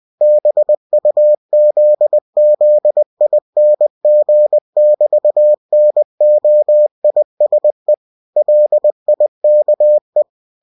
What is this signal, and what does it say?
Morse code, heard as a keyed tone, about 20 wpm, 600 Hz. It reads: BUZZING<BT>NOISE LIKE
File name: cw-books / winnie-the-pooh / winnie-20wpm-page-0002